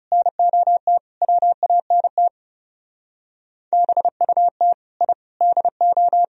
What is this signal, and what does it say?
Morse code, heard as a keyed tone, about 37 wpm, 700 Hz. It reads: NOT WANT BUT I DO
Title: Speed-Racing - Sets of 5 Words 30wpm